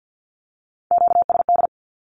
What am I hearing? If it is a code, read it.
Morse code, heard as a keyed tone, about 48 wpm, 700 Hz. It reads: QSD
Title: Q-Codes Repeated 3x 50wpm